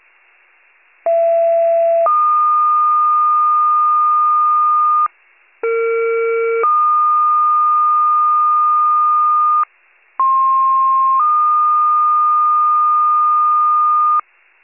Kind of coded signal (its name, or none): two-tone page